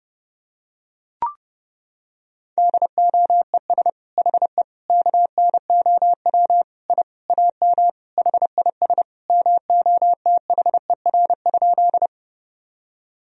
Morse code, heard as a keyed tone, about 30 wpm, 700 Hz. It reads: DOES HE KNOW I AM HIS MOTHER?
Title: Sentences from Top 1000 Words - Review Part 2 - 30wpm